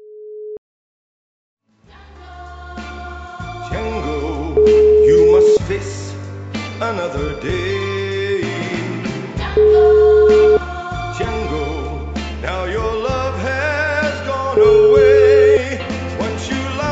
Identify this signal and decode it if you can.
Ringback tone (Europe, 425 Hz), with speech.